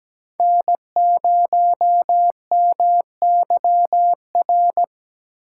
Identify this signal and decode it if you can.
Morse code, heard as a keyed tone, about 20 wpm, 700 Hz. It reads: N0MYR
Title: Speed Racing - USA 1x3 Call Signs 17wpm